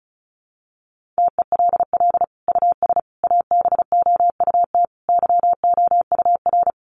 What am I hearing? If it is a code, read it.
Morse code, heard as a keyed tone, about 34 wpm, 700 Hz. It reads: TELL US ABOUT YOUR
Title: Sentences from Top 1400 Words 35wpm